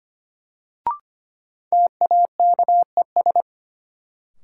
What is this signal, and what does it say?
Morse code, heard as a keyed tone, about 25 wpm, 700 Hz. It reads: TAKES
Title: Sets of 5 Words Encore - New Word Drill - 25wpm